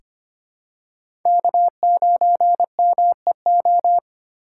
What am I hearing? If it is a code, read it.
Morse code, heard as a keyed tone, about 25 wpm, 700 Hz. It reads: K9MEO
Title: USA Call Signs - 4x Character Spacing - 25wpm